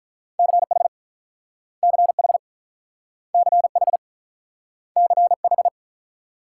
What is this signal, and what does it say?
Morse code, heard as a keyed tone, about 48 wpm, 700 Hz. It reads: CH CH CH CH